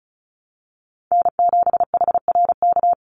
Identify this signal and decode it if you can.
Morse code, heard as a keyed tone, about 34 wpm, 700 Hz. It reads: N7HRK
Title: Call Signs - 35wpm